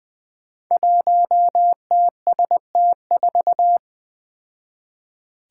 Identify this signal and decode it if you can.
Morse code, heard as a keyed tone, about 20 wpm, 700 Hz. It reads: 1TST4
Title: Five Challenging Letter-Number Combinations 20wpm